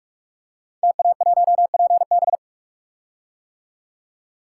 Morse code, heard as a keyed tone, about 44 wpm, 700 Hz. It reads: TA1PB
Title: Code Groups - 5 Letter:Numbers - 45wpm